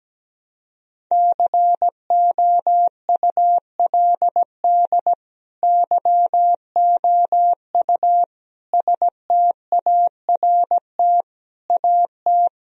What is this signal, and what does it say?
Morse code, heard as a keyed tone, about 17 wpm, 700 Hz. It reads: COULD YOU START AT
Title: Sentences from Top 1000 Words - Review Part 1 - 17wpm